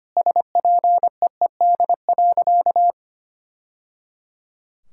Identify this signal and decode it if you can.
Morse code, heard as a keyed tone, about 25 wpm, 700 Hz. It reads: SPEED.
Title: Sentences from Top 1000 Words - Review Part 5 - 25wpm